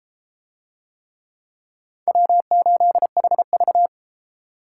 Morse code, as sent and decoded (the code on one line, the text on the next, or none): .-- ---.. .... ...-
W8HV